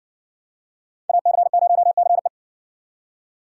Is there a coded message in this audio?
Yes: Morse code from 1.1 to 2.3 s, heard as a keyed tone, about 53 wpm, 700 Hz. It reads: AC0CE